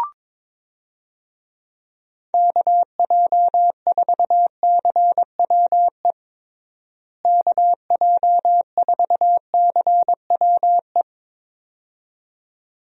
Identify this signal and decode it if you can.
Morse code, heard as a keyed tone, about 22 wpm, 700 Hz. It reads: KJ4CWE KJ4CWE
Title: USA Call Signs - 6 Characters - Repeated 2x - 22wpm